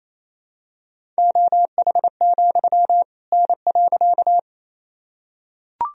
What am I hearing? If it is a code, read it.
Morse code, heard as a keyed tone, about 28 wpm, 700 Hz. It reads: OH, N.
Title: Sentences from Top 500 Words and Abbreviations 28wpm